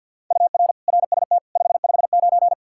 Morse code, heard as a keyed tone, about 53 wpm, 700 Hz. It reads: UR RST 559
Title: Practice QSO - Exchange 1 - Beginning - 50wpm